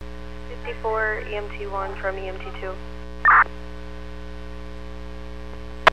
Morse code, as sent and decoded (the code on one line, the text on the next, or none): none